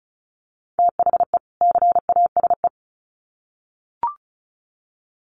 Morse code, heard as a keyed tone, about 34 wpm, 700 Hz. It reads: THE CASE
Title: Sets of 2 Words 35wpm